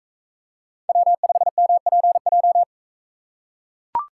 Morse code, heard as a keyed tone, about 42 wpm, 700 Hz. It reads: W5MPJ